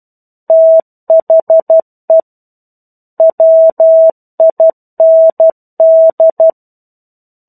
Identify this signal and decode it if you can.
Morse code, heard as a keyed tone, about 12 wpm, 650 Hz. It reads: THE WIND
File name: 01-Chapter-01 12 WPM